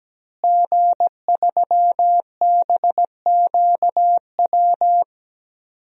Morse code, heard as a keyed tone, about 17 wpm, 700 Hz. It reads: G3BQW
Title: DXCC-100 Call Signs - 5 Characters - Repeated 2x - 17wpm